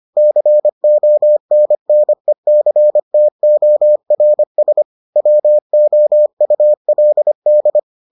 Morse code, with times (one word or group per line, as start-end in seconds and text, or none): 0.2-4.8 CONNECTORS
5.2-7.8 WOULD